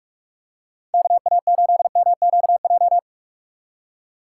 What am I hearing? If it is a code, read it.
Morse code, heard as a keyed tone, about 41 wpm, 700 Hz. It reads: KA8MQJ